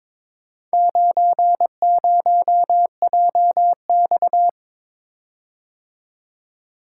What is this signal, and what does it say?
Morse code, heard as a keyed tone, about 22 wpm, 700 Hz. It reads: 90JX
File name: practice / 22wpm / Four Letter-Number Combinations - 22wpm